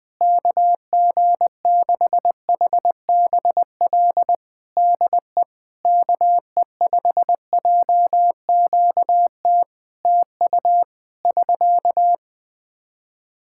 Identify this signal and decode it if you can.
Morse code, heard as a keyed tone, about 21 wpm, 700 Hz. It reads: KG6HBL DE KE5JQT TU <SK>
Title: Practice QSO - Exchange 4 - Ending - 20wpm